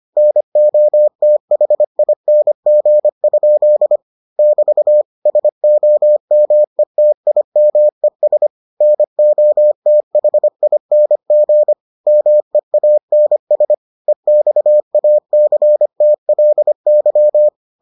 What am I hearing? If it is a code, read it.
Morse code, heard as a keyed tone, about 25 wpm, 600 Hz. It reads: NOTHING? <BT> SOMETIMES NOTHING MEANS EXACTLY